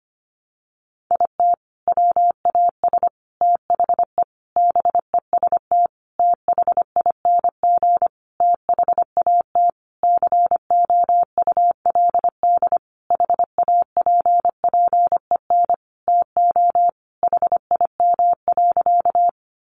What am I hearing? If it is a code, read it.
Morse code, heard as a keyed tone, about 25 wpm, 700 Hz. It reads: IT WAS THE BEST THING THAT COULD HAPPEN TO HIM.
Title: Sentences from Top 400 Words 25wpm